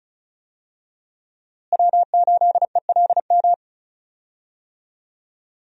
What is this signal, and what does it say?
Morse code, heard as a keyed tone, about 36 wpm, 700 Hz. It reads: W8ELM
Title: USA Call Signs - 5 Characters - Repeated 2x - 35wpm